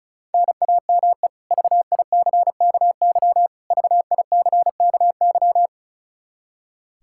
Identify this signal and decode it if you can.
Morse code, heard as a keyed tone, about 34 wpm, 700 Hz. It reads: NAME VICKY VICKY
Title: K1USN SST QSO Practice 35wpm